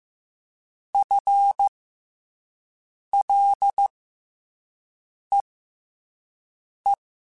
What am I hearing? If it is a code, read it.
Morse code, heard as a keyed tone, about 15 wpm, spaced as for 5 wpm, 800 Hz. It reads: FLEE
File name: FL / FL mp3s / FL Prefixes 3x2 sp @5